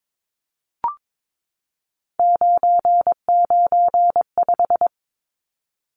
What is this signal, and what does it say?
Morse code, heard as a keyed tone, about 22 wpm, 700 Hz. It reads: 995
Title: Numbers - 3-Digit 22wpm